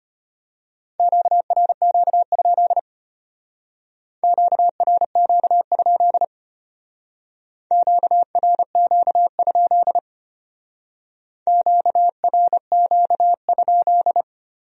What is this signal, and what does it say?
Morse code, heard as a keyed tone, about 30 wpm, 700 Hz. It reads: QRQ? QRQ? QRQ? QRQ?
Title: Speed-Racing Q-Codes - 25wpm